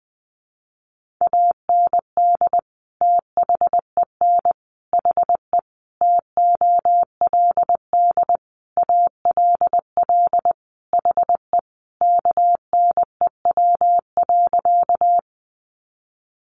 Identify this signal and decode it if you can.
Morse code, heard as a keyed tone, about 19 wpm, 700 Hz. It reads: AND THEN HE TOLD ALL HE KNEW.
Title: Sentences from Top 400 Words 20wpm